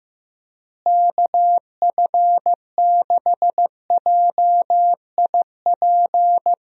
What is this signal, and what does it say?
Morse code, heard as a keyed tone, about 15 wpm, 700 Hz. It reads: KF6JIP
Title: USA Call Signs - 6 Characters - Repeated 3x - 15wpm